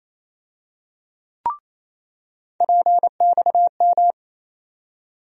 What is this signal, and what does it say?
Morse code, heard as a keyed tone, about 28 wpm, 700 Hz. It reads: PXM
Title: USA Call Sign Postfixes - 3 Letters 28wpm